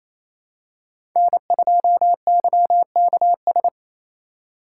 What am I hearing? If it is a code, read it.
Morse code, heard as a keyed tone, about 28 wpm, 690 Hz. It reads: N2YKS
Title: USA Call Signs - 5 Characters - 4x Character Spacing - 28wpm